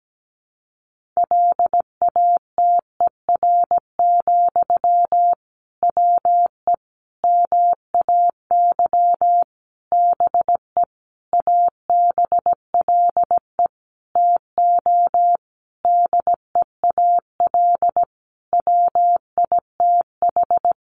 Morse code, heard as a keyed tone, about 17 wpm, 700 Hz. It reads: LATER, WE MAY BE ABLE TO DEAL WITH
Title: Sentences from Top 800 Words 17wpm